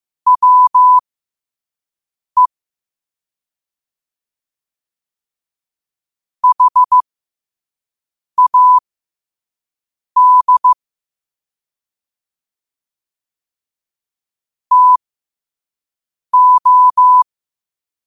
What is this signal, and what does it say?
Morse code, heard as a keyed tone, about 15 wpm, spaced as for 5 wpm, 1000 Hz. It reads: WE HAD TO
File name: qotdfiles / 5wpm